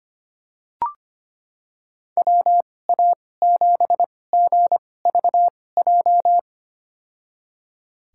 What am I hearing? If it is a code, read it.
Morse code, heard as a keyed tone, about 25 wpm, spaced as for 18 wpm, 700 Hz. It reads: WA7GVJ
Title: USA Call Signs - 2x Character Spacing - 25wpm